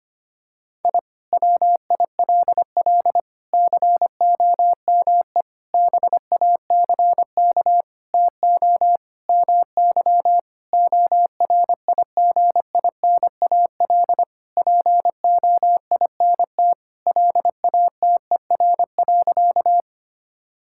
Morse code, heard as a keyed tone, about 25 wpm, 700 Hz. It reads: I WILL COME BACK TO MY ORIGINAL POINT LATER.
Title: Sentences from Top 1000 Words - Review Part 6 - 25wpm